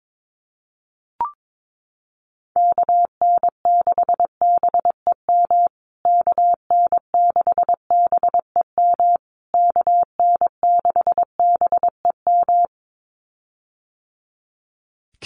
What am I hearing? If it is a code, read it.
Morse code, heard as a keyed tone, about 22 wpm, 700 Hz. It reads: KN6BEM KN6BEM KN6BEM
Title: USA Call Signs Repeated 3x 22wpm